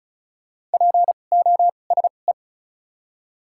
Morse code, heard as a keyed tone, about 34 wpm, 700 Hz. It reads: P O S E
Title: 2x Character Spacing - Top 500 Words - 35wpm